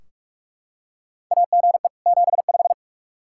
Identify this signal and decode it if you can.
Morse code, heard as a keyed tone, about 46 wpm, 700 Hz. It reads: AGE 75
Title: K1USN SST QSO Practice 45wpm